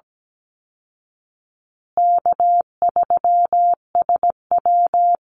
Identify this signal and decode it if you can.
Morse code, heard as a keyed tone, about 17 wpm, 700 Hz. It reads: K3SW
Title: Easy Call Signs Repeated 2x - 17wpm